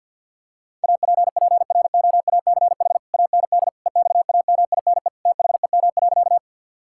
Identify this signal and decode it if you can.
Morse code, heard as a keyed tone, about 52 wpm, 700 Hz. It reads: APPROACH AND EXAMINE THEM.
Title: Sentences from Top 1600 Words 50wpm